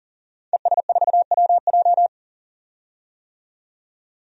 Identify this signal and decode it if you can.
Morse code, heard as a keyed tone, about 40 wpm, 700 Hz. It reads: ES4WJ
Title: DXCC-100 Call Signs - 5 Characters - 40wpm